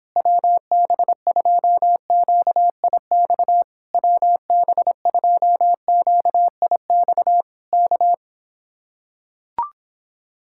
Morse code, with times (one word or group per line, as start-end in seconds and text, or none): 0.2-3.6 WB2QIX
3.9-7.4 WB2QIX
7.7-8.1 K